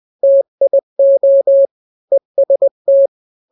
Morse code, heard as a keyed tone, about 20 wpm, 550 Hz. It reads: TIO EST
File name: n-grams / 3-grams 20 18 WPM